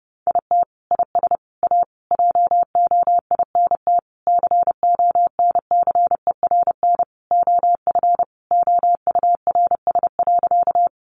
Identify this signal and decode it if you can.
Morse code, heard as a keyed tone, about 30 wpm, 700 Hz. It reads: IT IS A JOINT CONCERN OF OURS.
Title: Sentences from Top 1300 Words 30wpm